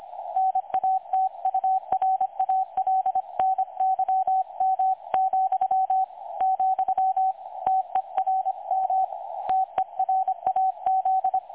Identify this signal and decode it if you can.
Morse code, heard as a keyed tone, about 25 wpm, 760 Hz. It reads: NATURALNYM, , TERM TERAZ
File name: dlazaawansowanych / nagraniazpasma / websdr recording 3565.0kHz